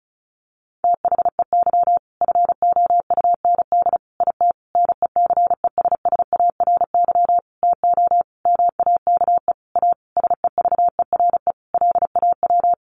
Morse code, heard as a keyed tone, about 34 wpm, 700 Hz. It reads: THEY FOUND IT NECESSARY TO MAKE A SEVERE LAW
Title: Sentences from Top 1400 Words 35wpm